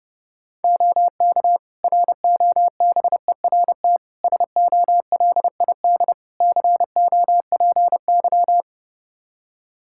Morse code, heard as a keyed tone, about 30 wpm, 700 Hz. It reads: OK ROBERT SOLID COPY
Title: K1USN SST QSO Practice 30wpm